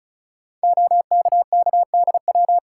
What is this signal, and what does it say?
Morse code, heard as a keyed tone, about 36 wpm, 700 Hz. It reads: OKKDW
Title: Code Groups - 5 Letter:Numbers - 35wpm